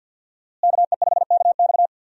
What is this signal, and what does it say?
Morse code, heard as a keyed tone, about 52 wpm, 700 Hz. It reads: KE5KX
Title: USA Call Signs - 5 Characters - Repeated 3x - 50wpm